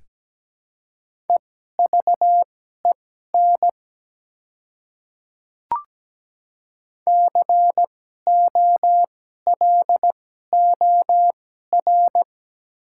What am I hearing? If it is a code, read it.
Morse code, heard as a keyed tone, about 17 wpm, spaced as for 12 wpm, 700 Hz. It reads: EVEN COLOR
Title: 2x Character Spacing - Top 300 Words - 17wpm